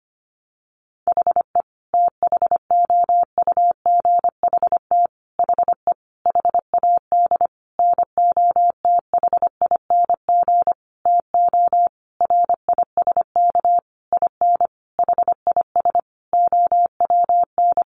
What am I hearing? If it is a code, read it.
Morse code, heard as a keyed tone, about 25 wpm, 700 Hz. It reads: HE THOUGHT HE HAD NOTHING TO RISK IN HIS OWN
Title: Sentences from Top 600 Words 25wpm